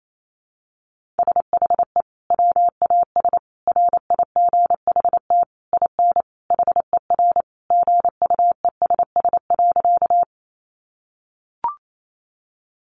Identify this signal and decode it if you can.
Morse code, heard as a keyed tone, about 28 wpm, 700 Hz. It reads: SHE WAS RIGHT IN HER GUESS.
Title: Sentences from Top 1500 Words 28wpm